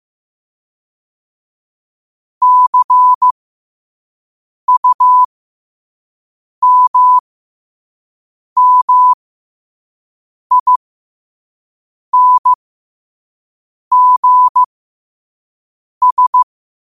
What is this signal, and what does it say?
Morse code, heard as a keyed tone, about 15 wpm, spaced as for 5 wpm, 1000 Hz. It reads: CUMMINGS